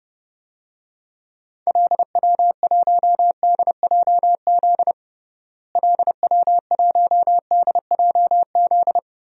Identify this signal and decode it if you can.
Morse code, heard as a keyed tone, about 30 wpm, 700 Hz. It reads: LW1DJZ LW1DJZ
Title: DXCC-100 Call Signs - 6 Characters - Repeated 2x - 30wpm